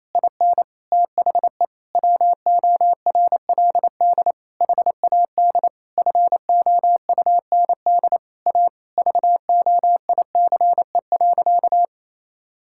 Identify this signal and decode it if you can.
Morse code, heard as a keyed tone, about 28 wpm, 700 Hz. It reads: IN THE WORLD HAD FOUND A VOICE.
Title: Sentences from Top 1600 Words 28wpm